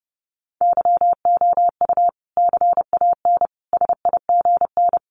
Morse code, heard as a keyed tone, about 30 wpm, 700 Hz. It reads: YOU CAN SIGN
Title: Sentences from Top 1000 Words 30wpm